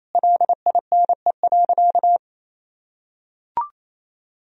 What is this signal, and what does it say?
Morse code, heard as a keyed tone, about 28 wpm, 700 Hz. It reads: LINE.
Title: Sentences from Top 1000 Words - Review Part 4 - 28wpm